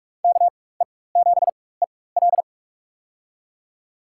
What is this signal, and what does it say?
Morse code, heard as a keyed tone, about 46 wpm, spaced as for 21 wpm, 700 Hz. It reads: KE7EL
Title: USA Call Signs - 5 Characters - 4x Character Spacing - 45wpm